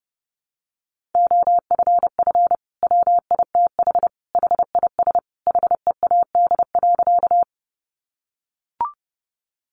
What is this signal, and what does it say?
Morse code, heard as a keyed tone, about 30 wpm, 700 Hz. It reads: OFF WITH HIS HEAD.
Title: Sentences from Top 300 Words 30wpm